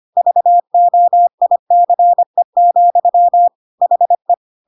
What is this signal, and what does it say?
Morse code, heard as a keyed tone, about 25 wpm, 700 Hz. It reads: VOICE, HE
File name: Pinocchio / Pinocchio ch7 25 25 ik1hge